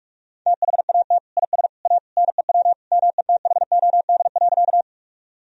Morse code, heard as a keyed tone, about 46 wpm, 700 Hz. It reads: THAT IS A NEW METHOD.
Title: Sentences from Top 1000 Words - Review Part 3 - 45wpm